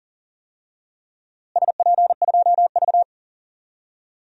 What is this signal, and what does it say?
Morse code, heard as a keyed tone, about 39 wpm, 700 Hz. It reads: SP2V
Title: DXCC-100 Call Signs - 4 Characters - Repeated 3x - 40wpm